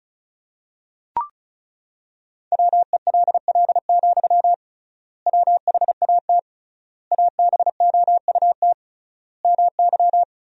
Morse code, heard as a keyed tone, about 36 wpm, 700 Hz. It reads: WELL, WHAT ABOUT MY
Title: 3x Word-Spacing - Sentences from Top 500 Words - 35wpm